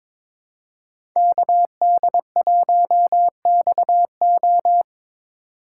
Morse code, heard as a keyed tone, about 26 wpm, 700 Hz. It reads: KD1XO
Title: Speed Racing - USA 2x2 Call Signs 22wpm